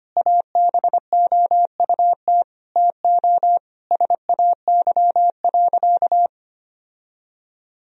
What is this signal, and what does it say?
Morse code, heard as a keyed tone, about 25 wpm, 700 Hz. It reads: ABOUT TO SAY.
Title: Sentences from Top 1000 Words - Review Part 2 - 25wpm